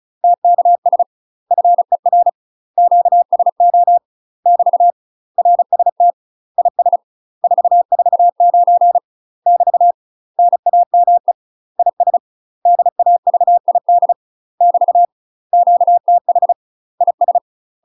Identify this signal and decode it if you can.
Morse code, heard as a keyed tone, about 36 wpm, 700 Hz. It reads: TKS FER QSO <BT> RST IS 449 <BT> NAME IS DAVID <BT> QTH IS
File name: UK MASTER QSOs TRAINING / 35WPM / UK MASTER QSOs 0035